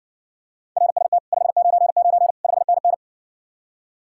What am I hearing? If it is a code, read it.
Morse code, heard as a keyed tone, about 60 wpm, 700 Hz. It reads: RST 599 5NN